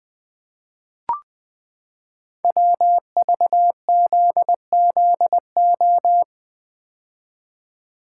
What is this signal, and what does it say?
Morse code, heard as a keyed tone, about 20 wpm, 700 Hz. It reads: WVZZO
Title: Code Groups - 5 Letters - 20wpm